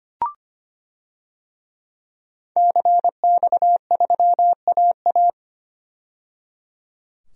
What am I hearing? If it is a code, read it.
Morse code, heard as a keyed tone, about 25 wpm, 700 Hz. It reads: CX3AA
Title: DXCC-100 Call Signs - 5 Characters - 25wpm